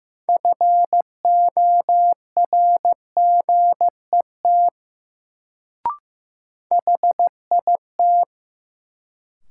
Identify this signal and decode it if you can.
Morse code, heard as a keyed tone, about 15 wpm, 700 Hz. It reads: FORGET HIT
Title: Sentences from Top 1200 Words - New Word Drill - 15wpm